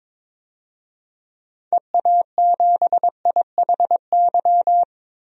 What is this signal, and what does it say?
Morse code, heard as a keyed tone, about 22 wpm, 700 Hz. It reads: EA7IHY
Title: DXCC-100 Call Signs - 6 Characters - 22wpm